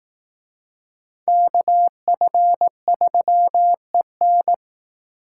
Morse code, heard as a keyed tone, about 18 wpm, 700 Hz. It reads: KF3EN
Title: Speed Racing - USA 2x2 Call Signs 15wpm